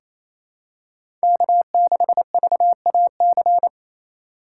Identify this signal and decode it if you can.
Morse code, heard as a keyed tone, about 28 wpm, 700 Hz. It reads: K6VAC
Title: USA Call Signs - 4x Character Spacing - 28wpm